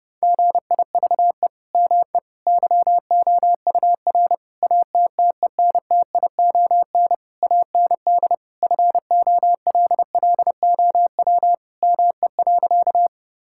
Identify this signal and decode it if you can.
Morse code, heard as a keyed tone, about 30 wpm, 700 Hz. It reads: GIVE ME YOUR ATTENTION AND FOLLOW ME.
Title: Sentences from Top 1000 Words - Review Part 6 - 30wpm